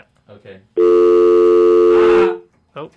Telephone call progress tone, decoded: dial tone